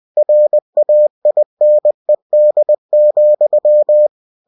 Morse code, heard as a keyed tone, about 20 wpm, 600 Hz. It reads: RAINED,
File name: cw-books / winnie-the-pooh / winnie-20wpm-page-0093